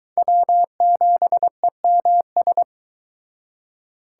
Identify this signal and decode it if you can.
Morse code, heard as a keyed tone, about 23 wpm, 700 Hz. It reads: W7EMS